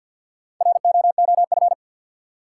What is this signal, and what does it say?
Morse code, heard as a keyed tone, about 53 wpm, 700 Hz. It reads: ROOF
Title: Four Letter Words - 50wpm